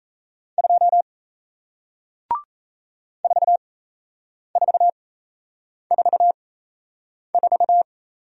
Morse code, heard as a keyed tone, about 37 wpm, spaced as for 8 wpm, 700 Hz. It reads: 2 4444